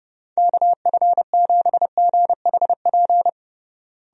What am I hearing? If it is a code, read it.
Morse code, heard as a keyed tone, about 30 wpm, 700 Hz. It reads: KF7GHP
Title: USA Call Signs - 6 Characters - Repeated 2x - 30wpm